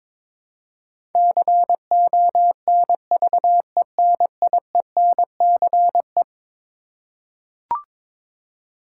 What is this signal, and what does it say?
Morse code, heard as a keyed tone, about 22 wpm, 700 Hz. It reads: CONVENIENCE
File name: practice / 22wpm / Eleven Letter Words - 22wpm